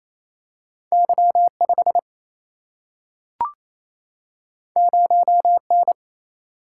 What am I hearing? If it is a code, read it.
Morse code, heard as a keyed tone, about 28 wpm, 700 Hz. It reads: Y5 0N
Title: Single-Letter Single-Number 28wpm